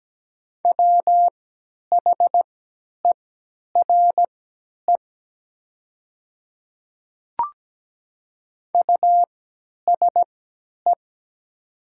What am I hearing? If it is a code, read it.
Morse code, heard as a keyed tone, about 17 wpm, spaced as for 10 wpm, 700 Hz. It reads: WHERE USE
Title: 3x Character Spacing - Top 300 Words - 17wpm